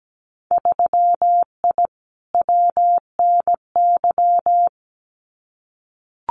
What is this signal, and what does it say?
Morse code, heard as a keyed tone, about 17 wpm, 700 Hz. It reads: 3I WNY
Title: Winter Field Day Exchanges Encore 17wpm